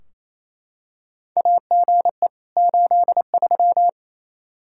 Morse code, heard as a keyed tone, about 28 wpm, 700 Hz. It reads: AGE 83
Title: Practice QSO - Exchange 3 - More Info - 28wpm